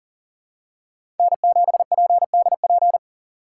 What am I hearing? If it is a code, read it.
Morse code, heard as a keyed tone, about 40 wpm, 700 Hz. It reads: N7PDP
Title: USA Call Signs - 5 Characters - 2x Character Spacing - 40wpm